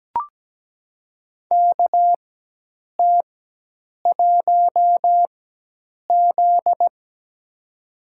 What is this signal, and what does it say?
Morse code, heard as a keyed tone, about 17 wpm, spaced as for 8 wpm, 700 Hz. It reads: KT1Z
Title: USA Call Signs - 4 Characters - 4x Character Spacing - 17wpm